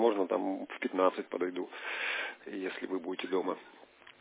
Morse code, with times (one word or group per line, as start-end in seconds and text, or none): none